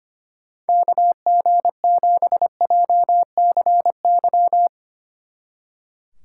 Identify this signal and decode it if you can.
Morse code, heard as a keyed tone, about 31 wpm, 700 Hz. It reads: KG7JCY
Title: Speed Racing - USA Call Signs 25wpm